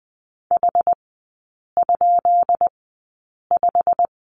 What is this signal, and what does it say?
Morse code, heard as a keyed tone, about 20 wpm, spaced as for 8 wpm, 690 Hz. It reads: H?5
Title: 133 Lesson 22e - Question Mark - Three Characters 20wpm